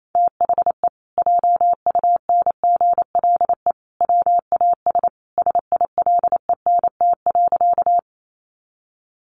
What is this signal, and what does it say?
Morse code, heard as a keyed tone, about 28 wpm, 700 Hz. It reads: THE JUNGLE WAS SILENT.
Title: A Sound Of Thunder 28wpm